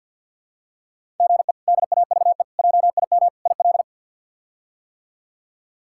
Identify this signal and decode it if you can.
Morse code, heard as a keyed tone, about 53 wpm, 700 Hz. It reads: GE DAVE JIM IL
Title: K1USN SST QSO Practice - 50wpm